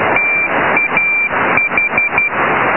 Morse code, heard as a keyed tone, about 12 wpm, 2350 Hz, with noise. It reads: TAH
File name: LZW 4217.8 0408